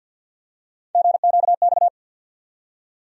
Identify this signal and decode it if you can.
Morse code, heard as a keyed tone, about 53 wpm, 700 Hz. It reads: GQX
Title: USA Call Sign Postfixes - 3 Letters 50wpm